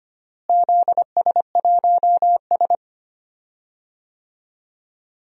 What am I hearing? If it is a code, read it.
Morse code, heard as a keyed tone, about 25 wpm, 700 Hz. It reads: ZS1S
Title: DXCC-100 Call Signs - 4 Characters - Repeated 2x - 25wpm